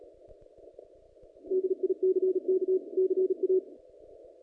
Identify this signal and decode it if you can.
Morse code, heard as a keyed tone, about 37 wpm, 370 Hz. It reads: DICK CA
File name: contests / naqp-cw-k5tr-op-k5pi- -0300